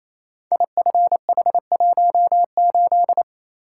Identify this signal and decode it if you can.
Morse code, heard as a keyed tone, about 27 wpm, 700 Hz. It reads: IFH18
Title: Code Groups - 5 Letter:Numbers - 28wpm